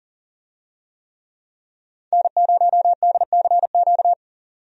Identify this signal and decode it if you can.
Morse code, heard as a keyed tone, about 40 wpm, 700 Hz. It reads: N0DCQ
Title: USA Call Signs - 5 Characters - 40wpm